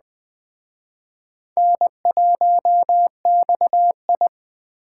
Morse code, heard as a keyed tone, about 22 wpm, 700 Hz. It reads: N1XI